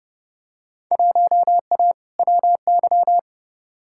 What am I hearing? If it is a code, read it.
Morse code, heard as a keyed tone, about 29 wpm, 700 Hz. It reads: 1A WY
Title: Field Day Exchanges - 30wpm